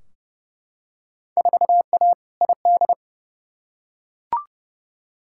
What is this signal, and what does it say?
Morse code, heard as a keyed tone, about 29 wpm, 700 Hz. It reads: 4A ID
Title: Field Day Exchanges - 30wpm